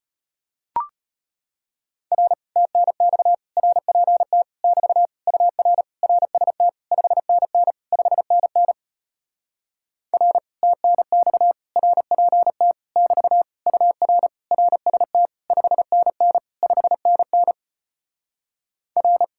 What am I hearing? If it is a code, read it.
Morse code, heard as a keyed tone, about 33 wpm, 700 Hz. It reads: R TNX RPT <BT> UR RST 5NN 5NN R TNX RPT <BT> UR RST 5NN 5NN R